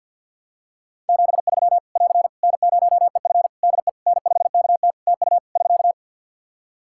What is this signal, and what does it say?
Morse code, heard as a keyed tone, about 50 wpm, 700 Hz. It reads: 73 <AR> N0EF DE N5KT TU <SK>
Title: Practice QSO - Exchange 4 - Ending - 50wpm